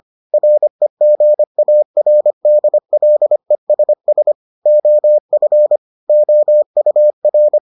Morse code, heard as a keyed tone, about 25 wpm, 600 Hz. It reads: REGARDLESS OF OUR